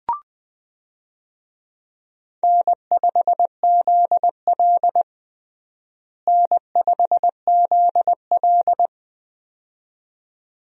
Morse code, heard as a keyed tone, about 20 wpm, 700 Hz. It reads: N5ZL N5ZL
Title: USA Call Signs - 4 Characters - Repeated 2x - 20wpm